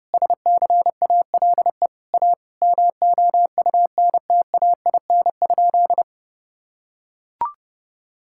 Morse code, heard as a keyed tone, about 30 wpm, 700 Hz. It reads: SCALE A MOUNTAIN?
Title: Sentences from Top 1000 Words - Review Part 5 - 30wpm